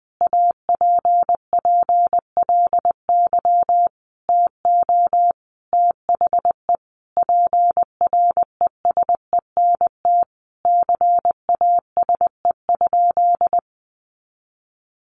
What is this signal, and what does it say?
Morse code, heard as a keyed tone, about 20 wpm, 700 Hz. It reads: APPLY TO THE PRESENT CASE?
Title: Sentences from Top 1200 Words 20wpm